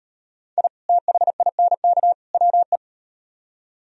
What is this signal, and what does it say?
Morse code, heard as a keyed tone, about 35 wpm, 700 Hz. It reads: I THINK WE